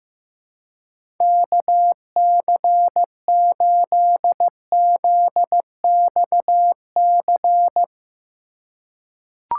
Morse code, heard as a keyed tone, about 15 wpm, 700 Hz. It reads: KC8ZXC
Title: USA Call Signs - 6 Characters - 3x Character Spacing - 15wpm